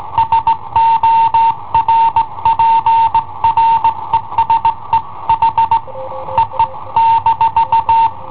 Morse code, heard as a keyed tone, about 17 wpm, 910 Hz. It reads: SORPRESE HEE -
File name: i7ohp / cassetta i7ohp B